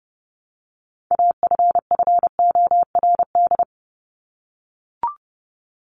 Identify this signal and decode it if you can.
Morse code, heard as a keyed tone, about 30 wpm, 700 Hz. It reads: AFFORD
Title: Top 1001-1250 Words 30wpm